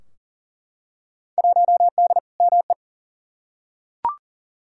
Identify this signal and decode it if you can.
Morse code, heard as a keyed tone, about 40 wpm, 700 Hz. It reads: 1D ME
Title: Field Day Exchanges Encore - 40wpm